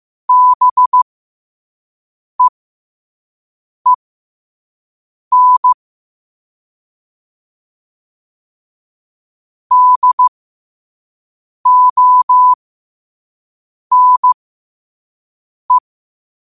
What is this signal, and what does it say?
Morse code, heard as a keyed tone, about 15 wpm, spaced as for 5 wpm, 1000 Hz. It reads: BEEN DONE
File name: qotdfiles / 5wpm